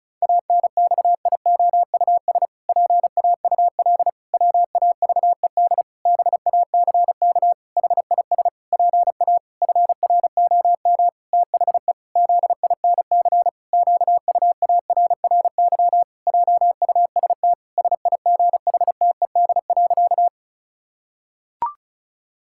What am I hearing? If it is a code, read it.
Morse code, heard as a keyed tone, about 36 wpm, 700 Hz. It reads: ANXIOUS PAUL WAVED BACK HIS PA FROM THE ZINC QUARRY JUST SIGHTED.
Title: Pangram Challenge v2 - 35wpm